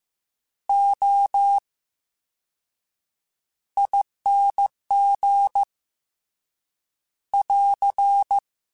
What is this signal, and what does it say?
Morse code, heard as a keyed tone, about 15 wpm, 800 Hz. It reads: O I N G <AR>
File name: Special INGs-mp3s / Special INGs-mp3s / VerySlowToFaster Ing @13